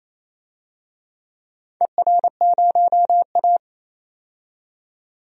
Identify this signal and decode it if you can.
Morse code, heard as a keyed tone, about 27 wpm, 690 Hz. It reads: ER0A
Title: DXCC-100 Call Signs - 4 Characters - 28wpm